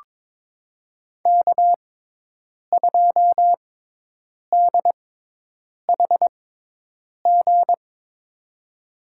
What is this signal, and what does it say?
Morse code, heard as a keyed tone, about 22 wpm, spaced as for 8 wpm, 700 Hz. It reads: K2DHG
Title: USA Call Signs - 5 Characters - 6x Character Spacing - 22wpm